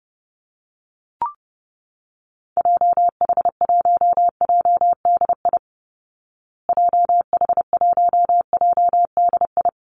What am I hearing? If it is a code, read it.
Morse code, heard as a keyed tone, about 28 wpm, 700 Hz. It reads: JH1JDI JH1JDI
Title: Call Signs Repeated 2x - 30wpm